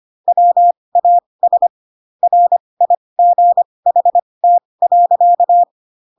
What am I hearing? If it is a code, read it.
Morse code, heard as a keyed tone, about 25 wpm, spaced as for 20 wpm, 700 Hz. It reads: WAS RIGHT.